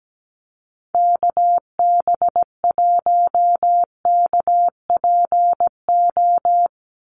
Morse code, heard as a keyed tone, about 17 wpm, 700 Hz. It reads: KB1KPO
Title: USA Call Signs 17wpm